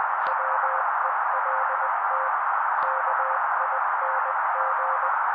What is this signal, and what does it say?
Morse code, heard as a keyed tone, about 20 wpm, 530 Hz. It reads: WELT KING
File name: daily / 18wpm noise